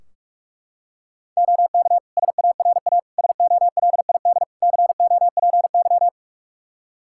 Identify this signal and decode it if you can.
Morse code, heard as a keyed tone, about 41 wpm, 700 Hz. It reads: OK SARA SOLID COPY
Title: Practice QSO - Exchange 3 - More Info - 45wpm